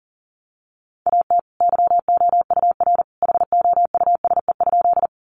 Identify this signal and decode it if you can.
Morse code, heard as a keyed tone, about 37 wpm, 700 Hz. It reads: AT YOUR HOUSE?
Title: Sentences from Top 200 Words 40wpm